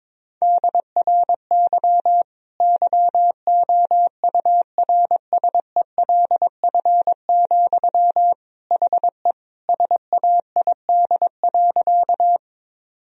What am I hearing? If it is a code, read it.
Morse code, heard as a keyed tone, about 22 wpm, 700 Hz. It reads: DRY YOURSELF, HE SAID.
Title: Sentences from Top 1000 Words - Review Part 3 - 22wpm